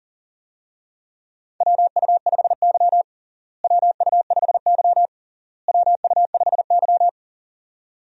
Morse code, heard as a keyed tone, about 37 wpm, 700 Hz. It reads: WU5Y WU5Y WU5Y
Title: USA Call Signs - 4 Characters - Repeated 3x - 40wpm